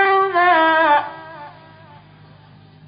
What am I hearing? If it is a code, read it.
There is no code.